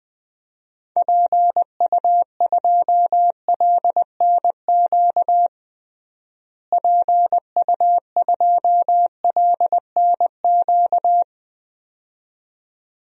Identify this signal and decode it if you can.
Morse code, heard as a keyed tone, about 20 wpm, 700 Hz. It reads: PU2LNQ PU2LNQ
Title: DXCC-100 Call Signs - 6 Characters - Repeated 2x - 20wpm